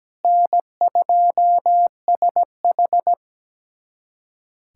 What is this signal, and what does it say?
Morse code, heard as a keyed tone, about 17 wpm, 700 Hz. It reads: N2SH
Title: USA Callsign DE Callsign - 4 Characters - 17wpm